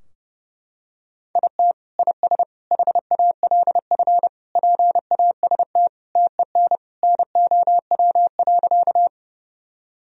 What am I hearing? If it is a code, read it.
Morse code, heard as a keyed tone, about 30 wpm, 700 Hz. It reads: IT IS HALF PAST TEN NOW.
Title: Sentences from Top 1000 Words - Review Part 3 - 30wpm